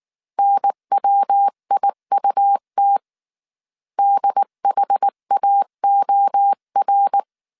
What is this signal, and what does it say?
Morse code, heard as a keyed tone, about 19 wpm, 800 Hz. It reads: NWIUT DHAOR
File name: CW / DET / 1 3 1500C S B 18wpm 1916